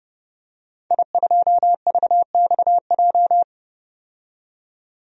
Morse code, heard as a keyed tone, about 32 wpm, 700 Hz. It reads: I2VXJ
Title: DXCC-100 Call Signs - 5 Characters - 30wpm